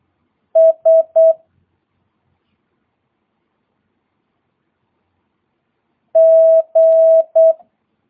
Morse code, heard as a keyed tone, about 8 wpm, 650 Hz. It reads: S G
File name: jh8cbh / 2022cw